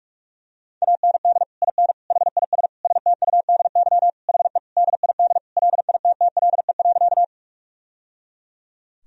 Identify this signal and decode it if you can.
Morse code, heard as a keyed tone, about 46 wpm, 700 Hz. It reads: AND IN HIS STUDY HE DID LITTLE.
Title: Sentences from Top 1000 Words - Review Part 1 - 45wpm